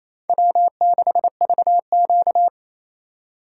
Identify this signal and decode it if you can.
Morse code, heard as a keyed tone, about 28 wpm, 700 Hz. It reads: W6VQ
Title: USA Call Signs - 4 Characters - 2x Character Spacing - 28wpm